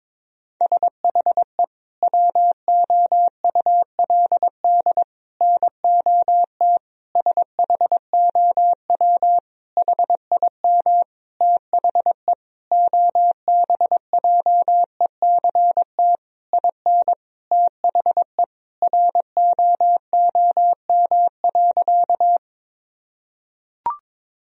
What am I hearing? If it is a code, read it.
Morse code, heard as a keyed tone, about 21 wpm, 700 Hz. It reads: SHE WOULD NOT SHOW HIM THE OBJECT IN THE ROOM.
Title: Sentences from Top 1000 Words - Review Part 3 - 22wpm